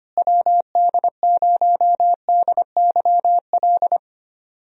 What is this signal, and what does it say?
Morse code, heard as a keyed tone, about 25 wpm, 700 Hz. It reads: WD0DYL
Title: USA Call Signs - 6 Characters - Repeated 2x - 25wpm